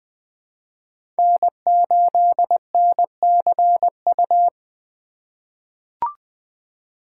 Morse code, heard as a keyed tone, about 20 wpm, 700 Hz. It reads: N8NCU
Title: USA Call Signs - 3x Character Spacing - 20wpm